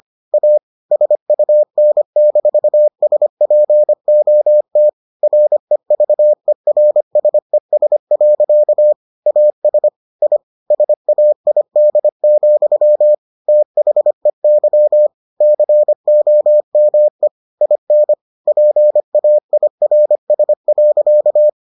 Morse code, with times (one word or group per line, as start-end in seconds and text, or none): 0.3-0.6 A
0.9-4.9 SUN-SPOT
5.2-8.9 REVERSES.
9.3-9.9 AS
10.2-10.4 I
10.7-13.1 SAID,
13.5-15.1 THEY
15.4-17.3 COME
17.6-18.1 IN
18.5-21.5 PAIRS.